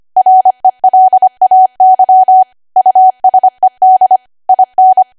Morse code, heard as a keyed tone, about 25 wpm, 750 Hz. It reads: RELAY USED IN
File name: cwa / W1AW-Level3 / 25WPM